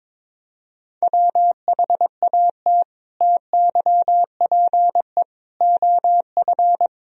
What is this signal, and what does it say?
Morse code, heard as a keyed tone, about 22 wpm, 700 Hz. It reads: WHAT TYPE OF
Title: Sentences from Top 1000 Words - Review Part 3 - 22wpm